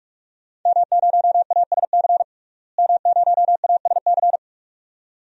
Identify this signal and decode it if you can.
Morse code, heard as a keyed tone, about 41 wpm, 700 Hz. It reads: M0ASC M0ASC
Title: DXCC-100 Call Signs - 5 Characters - Repeated 2x - 45wpm